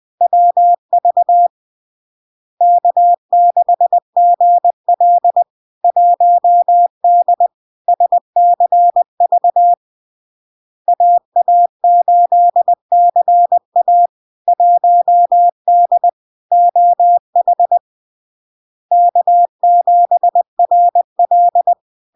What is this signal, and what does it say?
Morse code, heard as a keyed tone, about 20 wpm, 700 Hz. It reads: WV K6GL 1D SCV AA8CA 1D OH K7RL
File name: LARK-FD-2022-Log-in-CW-20wpm-700-Hz